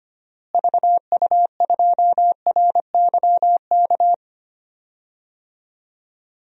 Morse code, heard as a keyed tone, about 25 wpm, 700 Hz. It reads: VU2RYK
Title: DXCC-100 Call Signs - 6 Characters - Repeated 3x - 25wpm